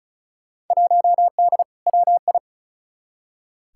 Morse code, heard as a keyed tone, about 36 wpm, 700 Hz. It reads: 1D WI
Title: Field Day Exchanges - 35wpm